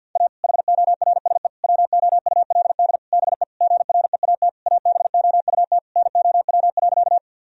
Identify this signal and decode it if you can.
Morse code, heard as a keyed tone, about 51 wpm, 700 Hz. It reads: A HORSE WOULD BE GREAT ABOUT NOW.
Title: Sentences from Top 1000 Words - Review Part 1 - 50wpm